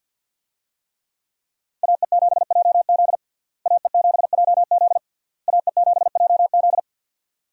Morse code, heard as a keyed tone, about 51 wpm, 700 Hz. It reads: AE7JZ AE7JZ AE7JZ
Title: USA Call Signs - 5 Characters - Repeated 3x - 50wpm